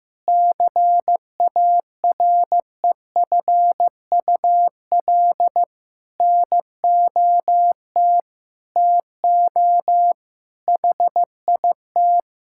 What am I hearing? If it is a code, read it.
Morse code, heard as a keyed tone, about 15 wpm, 700 Hz. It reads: CAREFUL NOT TO HIT
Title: Sentences from Top 1600 Words 15wpm